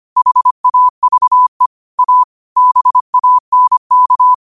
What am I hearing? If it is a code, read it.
Morse code, heard as a keyed tone, about 25 wpm, 1000 Hz. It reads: HAVE A BANK